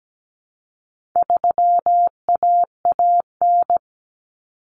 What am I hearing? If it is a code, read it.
Morse code, heard as a keyed tone, about 17 wpm, 700 Hz. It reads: 3AAN
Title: Common Cut Numbers - 4-Digit 17wpm